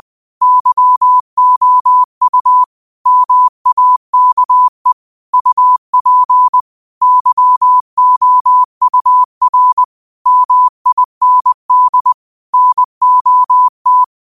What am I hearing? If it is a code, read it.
Morse code, heard as a keyed tone, about 20 wpm, 1000 Hz. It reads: YOU MAKE UP YOUR MIND NOT